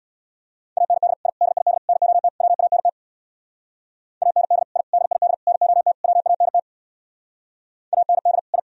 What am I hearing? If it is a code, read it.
Morse code, heard as a keyed tone, about 65 wpm, 700 Hz. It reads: AND I DID NOT WANT AND I DID NOT WANT AND I